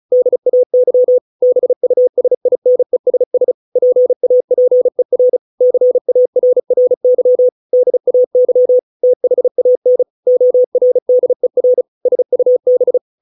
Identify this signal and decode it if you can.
Morse code, heard as a keyed tone, about 35 wpm, 500 Hz. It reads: DAY BUSINESS PAPER CARRY DAY THAN ORDER SUB